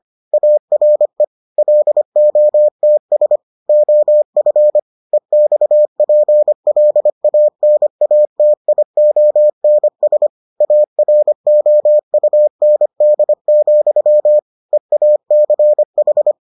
Morse code, heard as a keyed tone, about 25 wpm, 600 Hz. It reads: ARE LOTS OF EXPLANATIONS AROUND, EACH